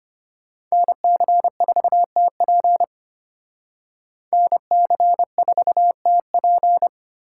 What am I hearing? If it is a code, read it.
Morse code, heard as a keyed tone, about 32 wpm, 700 Hz. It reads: NC4TP NC4TP